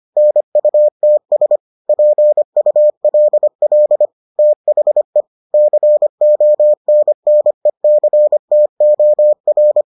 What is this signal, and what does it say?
Morse code, heard as a keyed tone, about 25 wpm, 600 Hz. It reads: NUTS PULL THE CONNECTOR